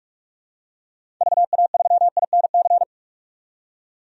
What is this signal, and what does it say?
Morse code, heard as a keyed tone, about 47 wpm, 700 Hz. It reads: VA3INC
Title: DXCC-100 Call Signs - 6 Characters - 45wpm